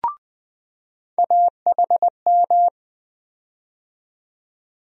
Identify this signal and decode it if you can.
Morse code, heard as a keyed tone, about 20 wpm, 700 Hz. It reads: AHM
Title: USA Call Sign Postfixes - 3 Letters 20wpm